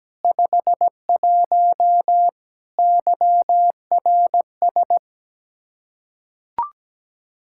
Morse code, heard as a keyed tone, about 17 wpm, 700 Hz. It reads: 51 YRS
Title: Practice QSO - Exchange 3 - More Info - 17wpm